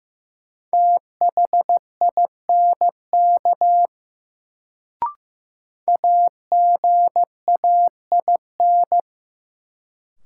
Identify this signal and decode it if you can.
Morse code, heard as a keyed tone, about 15 wpm, 700 Hz. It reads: THINK AGAIN
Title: Sentences from Top 200 Words - New Word Drill - 15wpm